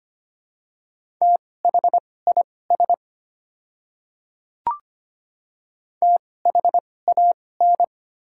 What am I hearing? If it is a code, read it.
Morse code, heard as a keyed tone, about 25 wpm, spaced as for 18 wpm, 690 Hz. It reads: THIS THAN
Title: 2x Character Spacing - Top 100 Words - 25wpm